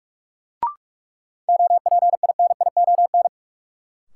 Morse code, heard as a keyed tone, about 46 wpm, 700 Hz. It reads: OPINION
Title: Sentences from Top 1200 Words - New Word Drill - 45wpm